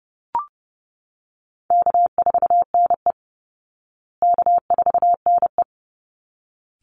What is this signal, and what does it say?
Morse code, heard as a keyed tone, about 30 wpm, 700 Hz. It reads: K4NE K4NE
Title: Call Signs Repeated 3x - 30wpm